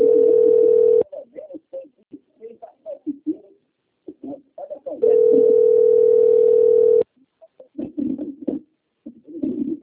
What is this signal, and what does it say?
Ringback tone (North America), 440 + 480 Hz.
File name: out- - - - - .16525